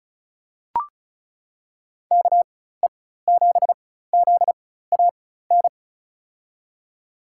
Morse code, heard as a keyed tone, about 34 wpm, spaced as for 16 wpm, 700 Hz. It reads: KE7ZAN
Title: USA Call Signs - 6 Characters - 4x Character Spacing - 35wpm